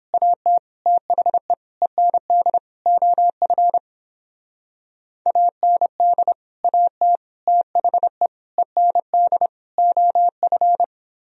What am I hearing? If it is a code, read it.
Morse code, heard as a keyed tone, about 31 wpm, 700 Hz. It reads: AT THE END OF AND AT THE END OF